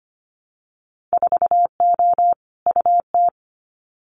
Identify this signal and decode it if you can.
Morse code, heard as a keyed tone, about 25 wpm, 690 Hz. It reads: 4O UT
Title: Winter Field Day Exchanges Encore 25wpm